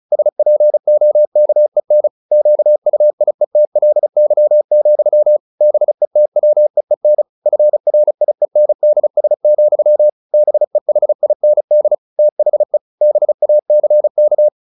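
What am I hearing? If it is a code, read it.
Morse code, heard as a keyed tone, about 34 wpm, 600 Hz. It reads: SPOKEN QUIETLY, BETWEEN FRIENDS, BEHIND THE BACK
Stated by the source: Morse practice at 35 wpm from CW med Gnister, sending Ken Follett's novel Never in English.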